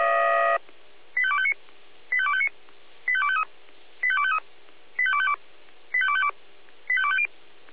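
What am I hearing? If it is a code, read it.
Five-tone selective call (ZVEI), heard as quick runs of five tones, tones of about 70 ms. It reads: 85259 85259 85253 85253 85252 85252 85250
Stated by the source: ZVEI five-tone call received on SDR.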